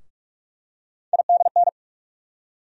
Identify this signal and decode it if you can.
Morse code, heard as a keyed tone, about 42 wpm, 700 Hz. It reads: IDN
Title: USA Call Sign Postfixes - 3 Letters 45wpm